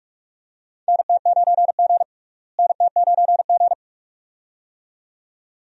Morse code, heard as a keyed tone, about 47 wpm, 700 Hz. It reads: NT9G NT9G
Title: USA Call Signs - 4 Characters - Repeated 2x - 45wpm